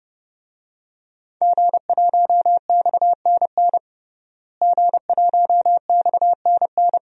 Morse code, heard as a keyed tone, about 30 wpm, 700 Hz. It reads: G1XNN G1XNN
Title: DXCC-100 Call Signs - 5 Characters - Repeated 2x - 30wpm